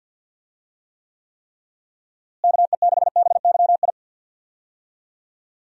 Morse code, heard as a keyed tone, about 46 wpm, 700 Hz. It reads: KE6BYI
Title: USA Call Signs - 6 Characters - Repeated 3x - 50wpm